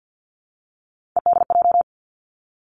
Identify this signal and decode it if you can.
Morse code, heard as a keyed tone, about 50 wpm, 700 Hz. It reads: EDJ
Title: Three Challenging Letter-Number Combinations 50wpm